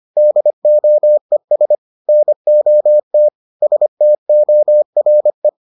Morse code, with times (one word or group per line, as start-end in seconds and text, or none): 0.2-1.8 DOES
2.1-3.3 NOT
3.6-5.5 STORE